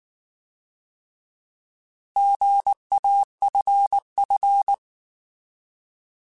Morse code, heard as a keyed tone, about 19 wpm, 800 Hz. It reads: GAFF